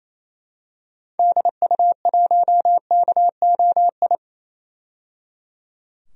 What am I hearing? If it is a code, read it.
Morse code, heard as a keyed tone, about 28 wpm, 700 Hz. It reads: DU1KOI
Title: DXCC-100 Call Signs - 6 Characters - 28wpm